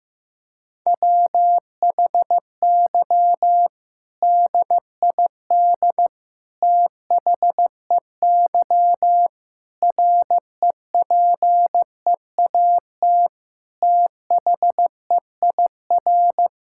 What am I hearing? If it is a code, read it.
Morse code, heard as a keyed tone, about 15 wpm, 700 Hz. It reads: WHY DID THEY REPEAT THEIR
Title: Sentences from Top 1600 Words 15wpm